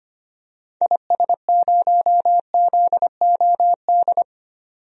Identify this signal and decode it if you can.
Morse code, heard as a keyed tone, about 25 wpm, 700 Hz. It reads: IS0ZOD
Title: DXCC-100 Call Signs - 6 Characters - 25wpm